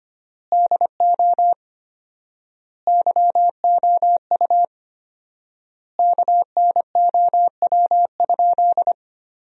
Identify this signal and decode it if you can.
Morse code, heard as a keyed tone, about 25 wpm, 700 Hz. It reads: DO YOU KNOW?
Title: 4x Word-Spacing - Sentences from Top 100 Words - 25wpm